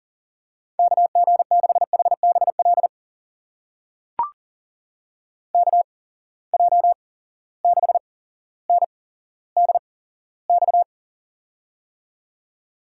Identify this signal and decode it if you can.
Morse code, heard as a keyed tone, about 39 wpm, spaced as for 17 wpm, 700 Hz. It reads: KG6HBL KJ6NDX
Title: USA Call Signs - 6 Characters - 8x Character Spacing - 40wpm